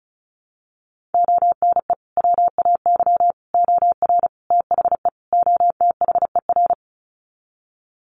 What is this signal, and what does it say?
Morse code, heard as a keyed tone, about 34 wpm, 700 Hz. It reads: ONE WAY OR THE OTHER
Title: Sets of 5 Words 35wpm